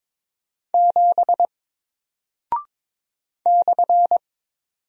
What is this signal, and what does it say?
Morse code, heard as a keyed tone, about 21 wpm, 700 Hz. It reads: 7 /
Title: Single Character (Letter:Number:Punctuation:Prosign) Alt 22wpm